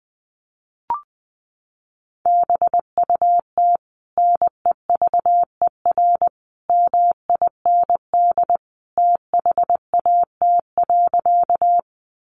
Morse code, heard as a keyed tone, about 20 wpm, 700 Hz. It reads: BUT NEVER MIND THAT.
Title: Sentences from Top 500 Words 20wpm